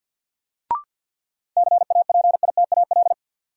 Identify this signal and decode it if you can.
Morse code, heard as a keyed tone, about 50 wpm, 700 Hz. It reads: CAPITAL
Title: Sentences from Top 1000 Words - New Word Drill - 50wpm